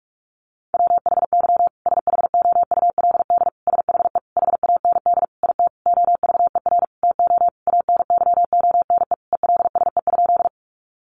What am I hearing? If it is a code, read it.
Morse code, heard as a keyed tone, about 44 wpm, 700 Hz. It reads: WHY SHOULD SHE HAND IT OVER TO ANYONE ELSE?
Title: Sentences from Top 900 Words 45wpm